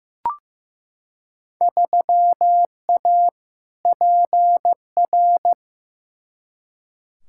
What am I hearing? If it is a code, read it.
Morse code, heard as a keyed tone, about 15 wpm, 700 Hz. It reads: 3A PR
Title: Field Day Exchanges Encore - 15wpm